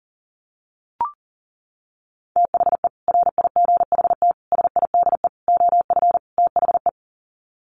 Morse code, heard as a keyed tone, about 40 wpm, 700 Hz. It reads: THE RIGHT SIDE OF THE
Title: Sets of 5 Words Encore 40wpm